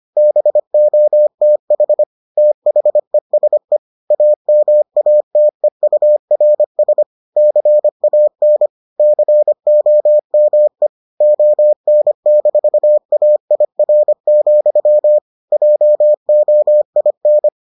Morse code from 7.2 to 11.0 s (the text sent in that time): CAN COME